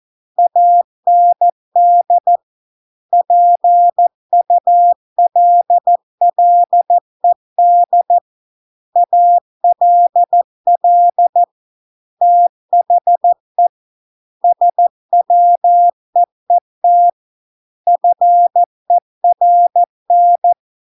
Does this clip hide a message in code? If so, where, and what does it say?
Yes: Morse code from 0.4 to 20.5 s, heard as a keyed tone, about 14 wpm, 700 Hz. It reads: AND PULLED ALL THE SWEET FERN